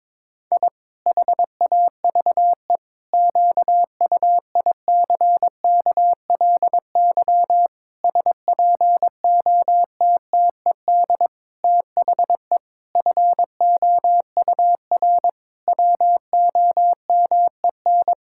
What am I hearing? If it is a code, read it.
Morse code, heard as a keyed tone, about 22 wpm, 700 Hz. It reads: I HAVE QUICKLY SPOTTED THE FOUR WOMEN
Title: Pangram Challenge v2 - 22wpm